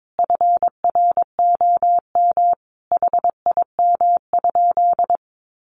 Morse code, heard as a keyed tone, about 22 wpm, 700 Hz. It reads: FROM HIM?
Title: Sentences from Top 1300 Words 22wpm